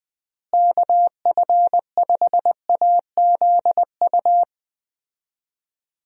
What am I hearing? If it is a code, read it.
Morse code, heard as a keyed tone, about 20 wpm, 700 Hz. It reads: KF5AZU
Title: USA Call Signs - 6 Characters - Repeated 2x - 20wpm